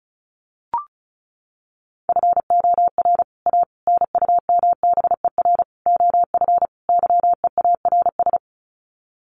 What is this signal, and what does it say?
Morse code, heard as a keyed tone, about 34 wpm, 700 Hz. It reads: FOR A NUMBER OF YEARS
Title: Sets of 5 Words Encore 35wpm